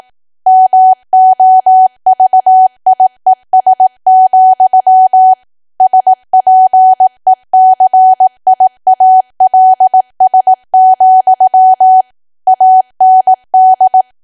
Morse code, heard as a keyed tone, about 18 wpm, 750 Hz. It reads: MOVIES, SPECIALS, AND
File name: Archive / 18 WPM / 18WPM